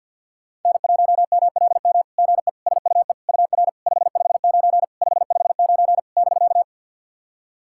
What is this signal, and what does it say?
Morse code, heard as a keyed tone, about 50 wpm, 700 Hz. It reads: N1MLM GE SUE UR 559 559 <BK>
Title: SOTA QSO Practice 50wpm